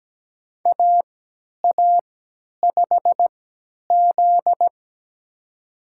Morse code, heard as a keyed tone, about 17 wpm, spaced as for 10 wpm, 700 Hz. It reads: AA5Z
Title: USA Call Signs - 3x Character Spacing - 17wpm